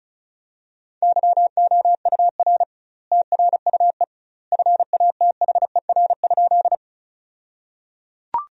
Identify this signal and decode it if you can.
Morse code, heard as a keyed tone, about 36 wpm, 700 Hz. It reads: YOUR TRUE FATHER?
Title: 2x Word-Spacing - Sentences from Top 400 Words - 35wpm